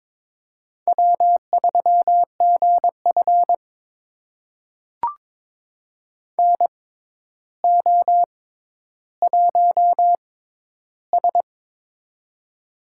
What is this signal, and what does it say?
Morse code, heard as a keyed tone, about 22 wpm, spaced as for 11 wpm, 700 Hz. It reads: W3GF NO1S
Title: USA Call Signs - 6x Character Spacing - 22wpm